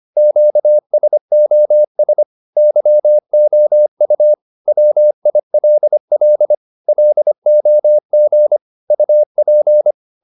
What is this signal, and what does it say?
Morse code, heard as a keyed tone, about 25 wpm, 600 Hz. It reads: QSOS YOU WILL LOG UP